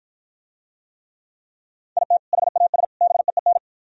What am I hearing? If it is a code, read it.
Morse code, heard as a keyed tone, about 52 wpm, 700 Hz. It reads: IT HAS BEEN